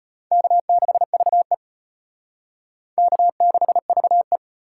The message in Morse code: K6VE K6VE

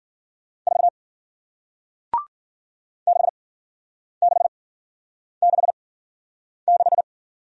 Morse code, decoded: 4 6666

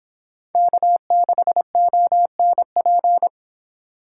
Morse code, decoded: K6ONP

